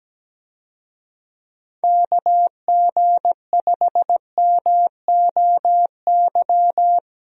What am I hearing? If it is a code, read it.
Morse code, heard as a keyed tone, about 17 wpm, 700 Hz. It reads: KG5MOY
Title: USA Call Signs - 6 Characters - Repeated 2x - 17wpm